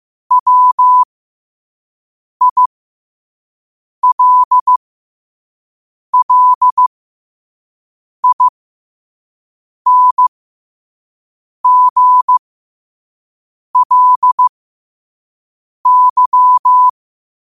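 Morse code, heard as a keyed tone, about 15 wpm, spaced as for 5 wpm, 1000 Hz. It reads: WILLINGLY
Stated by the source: Morse quote of the day at 5 wpm.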